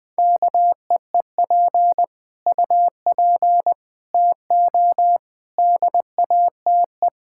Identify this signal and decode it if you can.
Morse code, heard as a keyed tone, about 20 wpm, 700 Hz. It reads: KEEP UP TO DATE
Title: Sentences with 9-Word Phrases - Spoken Beginning - 20wpm